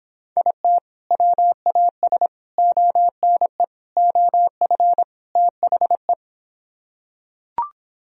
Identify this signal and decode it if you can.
Morse code, heard as a keyed tone, about 26 wpm, 700 Hz. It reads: IT WAS ONE OF THE